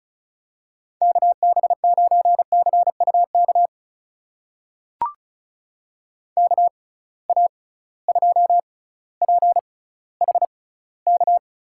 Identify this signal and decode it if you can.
Morse code, heard as a keyed tone, about 34 wpm, spaced as for 18 wpm, 700 Hz. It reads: KB9CUK KA2PHK
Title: USA Call Signs - 6 Characters - 6x Character Spacing - 35wpm